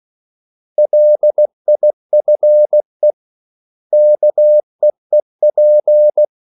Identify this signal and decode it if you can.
Morse code, heard as a keyed tone, about 16 wpm, 600 Hz. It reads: LIFE KEEP
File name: CW / Practice / 500MostCommonEnglishWords16